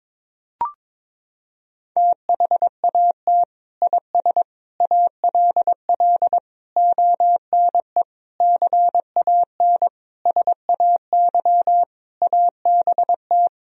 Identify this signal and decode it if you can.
Morse code, heard as a keyed tone, about 22 wpm, 700 Hz. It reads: THAT IS ALL ONE CAN SAY ABT
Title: Sentences from Top 200 Words and Abbreviations 22wpm